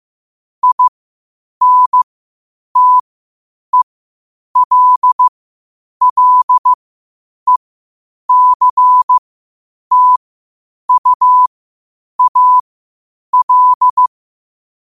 Morse code, heard as a keyed tone, about 15 wpm, spaced as for 8 wpm, 1000 Hz. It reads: INTELLECTUAL
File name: qotdfiles / 8wpm